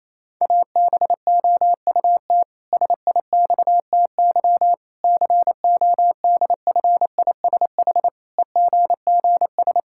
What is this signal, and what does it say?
Morse code, heard as a keyed tone, about 28 wpm, 700 Hz. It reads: ABOUT SIXTY CODFISH EGGS
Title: Pangram Challenge v2 - 28wpm